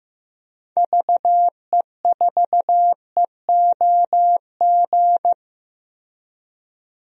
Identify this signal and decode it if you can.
Morse code, heard as a keyed tone, about 16 wpm, 700 Hz. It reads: VE4EOG
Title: DXCC-100 Call Signs - 6 Characters - Repeated 3x - 15wpm